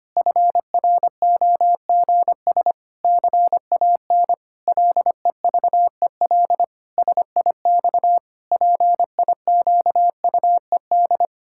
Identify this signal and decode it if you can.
Morse code, heard as a keyed tone, about 25 wpm, 700 Hz. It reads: FROGS CAN LEVEL SIX PIQUED
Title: Pangram Challenge v2 - 25wpm